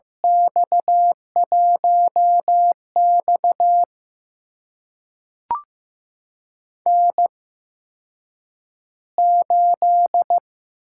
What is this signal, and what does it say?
Morse code, heard as a keyed tone, about 15 wpm, spaced as for 8 wpm, 700 Hz. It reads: X1X N8
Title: USA Call Signs - 4 Characters - 8x Character Spacing - 15wpm